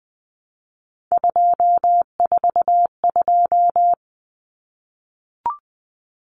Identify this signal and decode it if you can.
Morse code, heard as a keyed tone, about 20 wpm, 700 Hz. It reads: 242
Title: Numbers - 3-Digit 20wpm